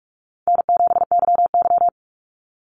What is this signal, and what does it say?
Morse code, heard as a keyed tone, about 44 wpm, 700 Hz. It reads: N7YY
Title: NA SOTA Chaser Callsigns 45wpm